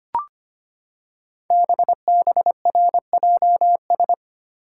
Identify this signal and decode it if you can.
Morse code, heard as a keyed tone, about 25 wpm, 700 Hz. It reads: BBRJS
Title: Code Groups - 5 Letters - 25wpm